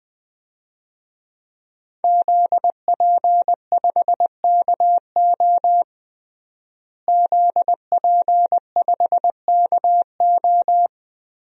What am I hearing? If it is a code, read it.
Morse code, heard as a keyed tone, about 21 wpm, 700 Hz. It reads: ZP5KO ZP5KO
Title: DXCC-100 Call Signs - 5 Characters - Repeated 3x - 20wpm